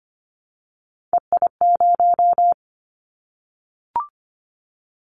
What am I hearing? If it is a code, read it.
Morse code, heard as a keyed tone, about 25 wpm, 700 Hz. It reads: EI0
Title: Callsign Prefixes for Top 50 DXCC - 25wpm